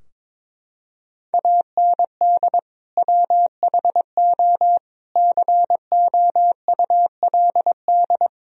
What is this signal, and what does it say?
Morse code, heard as a keyed tone, about 22 wpm, 700 Hz. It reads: AND WHO COULD
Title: Sentences from Top 1000 Words - Review Part 3 - 22wpm